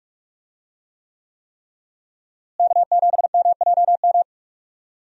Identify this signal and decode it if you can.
Morse code, heard as a keyed tone, about 44 wpm, 700 Hz. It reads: K7MJM
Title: USA Call Signs - 5 Characters - Repeated 3x - 45wpm